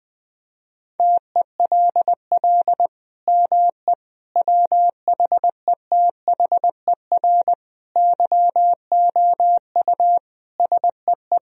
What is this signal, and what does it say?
Morse code, heard as a keyed tone, about 20 wpm, 700 Hz. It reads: TELL ME WHETHER YOU SEE
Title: Sentences from Top 1000 Words - Review Part 3 - 20wpm